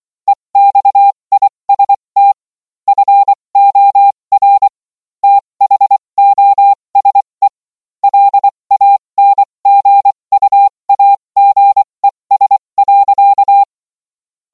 Morse code, heard as a keyed tone, about 24 wpm, spaced as for 20 wpm, 800 Hz. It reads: EXIST FOR THOSE LANGUAGES.